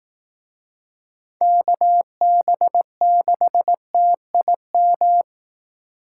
Morse code, heard as a keyed tone, about 18 wpm, 700 Hz. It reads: KB6TIM
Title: Speed Racing - USA Call Signs 15wpm